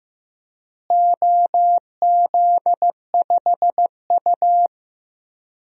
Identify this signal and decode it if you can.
Morse code, heard as a keyed tone, about 15 wpm, 700 Hz. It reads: OZ5U
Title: DXCC-100 Call Signs - 4 Characters - Repeated 3x - 15wpm